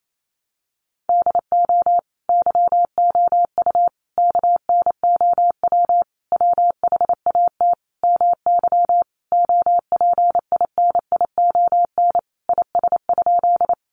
Morse code, heard as a keyed tone, about 28 wpm, 700 Hz. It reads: DO YOU KNOW WHAT MY OPINION IS?
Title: Sentences from Top 1200 Words 28wpm